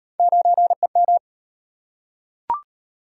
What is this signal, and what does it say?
Morse code, heard as a keyed tone, about 35 wpm, 700 Hz. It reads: 9EM